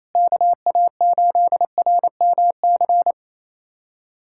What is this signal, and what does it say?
Morse code, heard as a keyed tone, about 28 wpm, 700 Hz. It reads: KA8RMC
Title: USA Callsign DE Callsign - 6 Characters - 28wpm